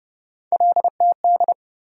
Morse code, heard as a keyed tone, about 30 wpm, 700 Hz. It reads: LTD
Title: USA Call Sign Postfixes - 3 Letters 30wpm